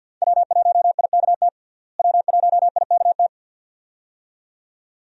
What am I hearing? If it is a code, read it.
Morse code, heard as a keyed tone, about 47 wpm, 700 Hz. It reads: W1IKT W1IKT
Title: USA Call Signs - 5 Characters - Repeated 2x - 50wpm